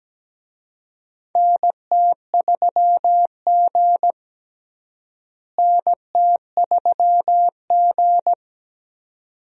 Morse code, heard as a keyed tone, about 17 wpm, 700 Hz. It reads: NT3G NT3G
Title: USA Call Signs - 4 Characters - Repeated 2x - 17wpm